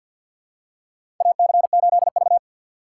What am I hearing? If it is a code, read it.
Morse code, heard as a keyed tone, about 51 wpm, 700 Hz. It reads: AC8V
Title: USA Call Signs - 3x Character Spacing - 50wpm